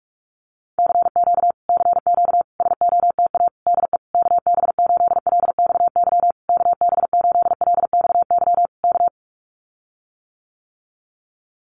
Morse code, heard as a keyed tone, about 44 wpm, 700 Hz. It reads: CQ CQ SOTA DE KB8LXY KB8LXY K
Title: Common QSO Phrases Encore 45wpm